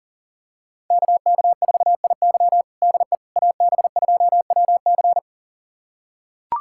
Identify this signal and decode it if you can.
Morse code, heard as a keyed tone, about 40 wpm, 700 Hz. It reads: KK4IY DE AB2WC
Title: USA Callsign DE Callsign - 5 Characters - 40wpm